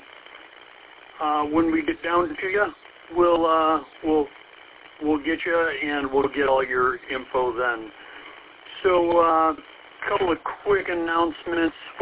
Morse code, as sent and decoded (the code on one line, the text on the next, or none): none